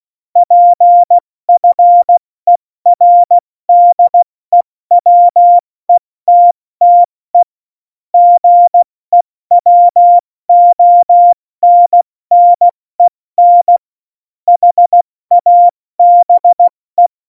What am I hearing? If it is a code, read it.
Morse code, heard as a keyed tone, about 16 wpm, 700 Hz. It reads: PFERDEWETTE GEWONNEN HABE
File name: Morsekurs / Witze / Witz 18 August 2020, 70 F80